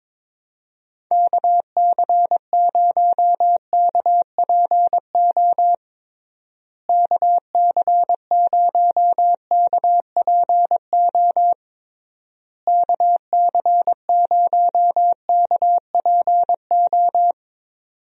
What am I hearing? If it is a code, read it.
Morse code, heard as a keyed tone, about 22 wpm, 700 Hz. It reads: KC0KPO KC0KPO KC0KPO
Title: USA Call Signs - 6 Characters - Repeated 3x - 22wpm